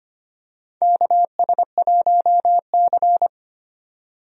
Morse code, heard as a keyed tone, about 25 wpm, 700 Hz. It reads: KS1C
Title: USA Call Signs - 4 Characters - 25wpm